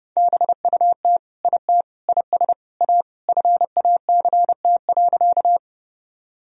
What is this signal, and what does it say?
Morse code, heard as a keyed tone, about 30 wpm, 700 Hz. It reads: BUT IT IS A FACT.
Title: Sentences from Top 1000 Words - Review Part 3 - 30wpm